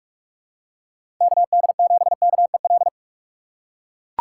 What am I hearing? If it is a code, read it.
Morse code, heard as a keyed tone, about 42 wpm, 700 Hz. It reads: KD7KEL
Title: USA Call Signs - 4x Character Spacing - 45wpm